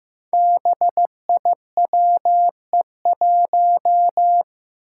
Morse code, heard as a keyed tone, about 15 wpm, 700 Hz. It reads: BIWE1
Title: Five Challenging Letter-Number Combinations 15wpm